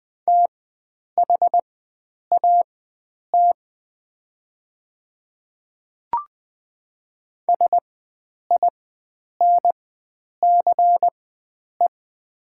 Morse code, heard as a keyed tone, about 20 wpm, spaced as for 9 wpm, 700 Hz. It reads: THAT SINCE
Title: 4x Character Spacing - Top 300 Words - 20wpm